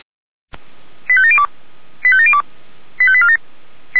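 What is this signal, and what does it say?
Five-tone selective call (ZVEI), heard as quick runs of five tones, tones of about 70 ms. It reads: 85702 85702 85847
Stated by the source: ZVEI five-tone call received on SDR.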